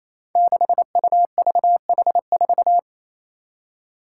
Morse code, heard as a keyed tone, about 28 wpm, 700 Hz. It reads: 6UVH4
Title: Five Challenging Letter-Number Combinations 28wpm